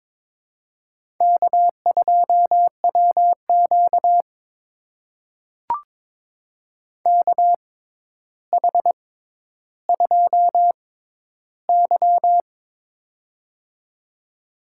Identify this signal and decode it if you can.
Morse code, heard as a keyed tone, about 23 wpm, spaced as for 11 wpm, 700 Hz. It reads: K2WQ KH2Y
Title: USA Call Signs - 4 Characters - 6x Character Spacing - 22wpm